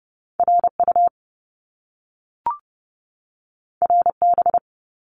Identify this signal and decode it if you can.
Morse code, heard as a keyed tone, about 30 wpm, 700 Hz. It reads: RU RB
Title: Top Two Letter Combinations in Words 30wpm